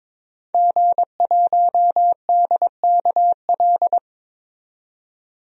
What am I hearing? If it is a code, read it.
Morse code, heard as a keyed tone, about 22 wpm, 700 Hz. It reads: G1DKL
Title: DXCC-100 Call Signs - 5 Characters - 22wpm